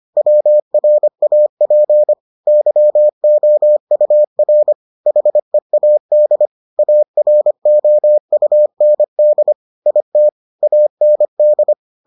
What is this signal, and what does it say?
Morse code, heard as a keyed tone, about 25 wpm, 600 Hz. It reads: WRAP YOUR HEAD AROUND IT AND